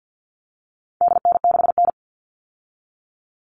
Morse code, heard as a keyed tone, about 50 wpm, 700 Hz. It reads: DN6N
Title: All Cut Numbers - 4-Digit 50wpm